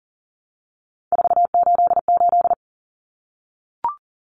Morse code, heard as a keyed tone, about 40 wpm, 700 Hz. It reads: 488
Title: Numbers - 3-Digit 40wpm